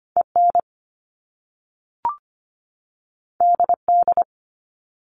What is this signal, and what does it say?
Morse code, heard as a keyed tone, about 25 wpm, 700 Hz. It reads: EN DD
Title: Top Two Letter Combinations in Words 25wpm